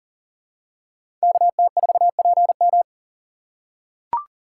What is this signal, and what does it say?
Morse code, heard as a keyed tone, about 40 wpm, 700 Hz. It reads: KT4PM
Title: USA Call Signs - 5 Characters - 2x Character Spacing - 40wpm